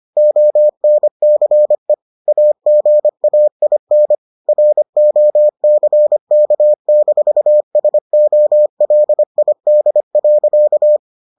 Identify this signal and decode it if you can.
Morse code, heard as a keyed tone, about 25 wpm, 600 Hz. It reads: ONCE AGAIN ROCK-SOLID.